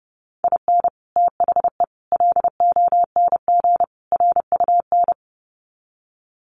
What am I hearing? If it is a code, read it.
Morse code, heard as a keyed tone, about 30 wpm, 700 Hz. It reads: IN THE LONG RUN
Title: Sets of 4 Words 30wpm